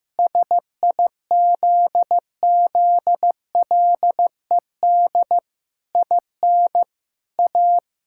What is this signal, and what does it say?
Morse code, heard as a keyed tone, about 15 wpm, 700 Hz. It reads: SIZZLED IN A
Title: Pangram Challenge v2 - 15wpm